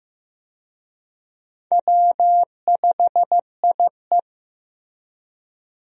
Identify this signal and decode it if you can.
Morse code, heard as a keyed tone, about 15 wpm, 700 Hz. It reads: W5IE
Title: USA Call Signs - 4 Characters - Repeated 3x - 15wpm